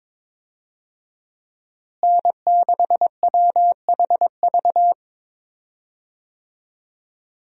Morse code, heard as a keyed tone, about 22 wpm, 700 Hz. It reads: N6WHV
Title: USA Call Signs - 5 Characters - Repeated 2x - 22wpm